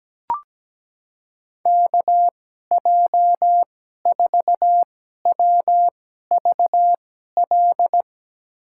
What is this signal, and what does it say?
Morse code, heard as a keyed tone, about 17 wpm, spaced as for 12 wpm, 700 Hz. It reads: KJ4WVL
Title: USA Call Signs - 6 Characters - 2x Character Spacing - 17wpm